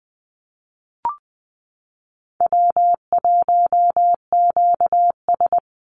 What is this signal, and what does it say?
Morse code, heard as a keyed tone, about 20 wpm, 700 Hz. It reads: W1QS
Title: Easy Call Signs Repeated 3x - 20wpm